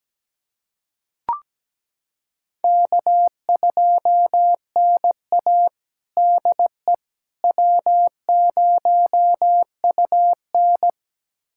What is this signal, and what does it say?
Morse code, heard as a keyed tone, about 17 wpm, 700 Hz. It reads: K2NA DE W0UN
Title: USA Callsign DE Callsign - 4 Characters - 17wpm